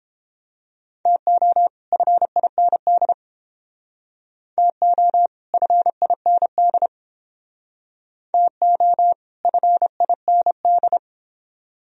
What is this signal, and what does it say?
Morse code, heard as a keyed tone, about 30 wpm, 700 Hz. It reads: TO FIND TO FIND TO FIND